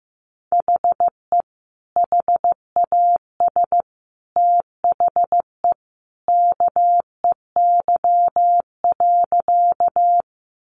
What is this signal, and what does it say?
Morse code, heard as a keyed tone, about 15 wpm, 700 Hz. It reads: HE HAS THE KEY.
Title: Sentences from Top 600 Words 15wpm